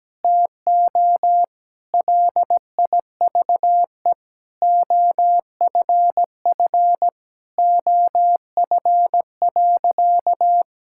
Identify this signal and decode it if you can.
Morse code, heard as a keyed tone, about 17 wpm, 700 Hz. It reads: TO LIVE OFF OF.
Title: Sentences from Top 1000 Words - Review Part 5 - 17wpm